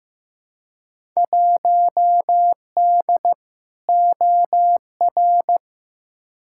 Morse code, heard as a keyed tone, about 15 wpm, 700 Hz. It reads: 1D OR
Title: Field Day Exchanges - 15wpm